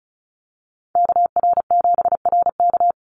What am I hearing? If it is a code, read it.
Morse code, heard as a keyed tone, about 34 wpm, 700 Hz. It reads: KR7RK
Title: NA SOTA Activator Callsigns 35wpm